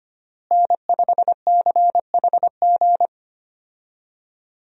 Morse code, heard as a keyed tone, about 25 wpm, 700 Hz. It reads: N5CHG
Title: USA Call Signs - 5 Characters - Repeated 2x - 25wpm